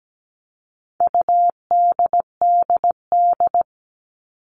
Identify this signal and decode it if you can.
Morse code, heard as a keyed tone, about 17 wpm, 700 Hz. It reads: UDDD
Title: All Cut Numbers - 4-Digit 17wpm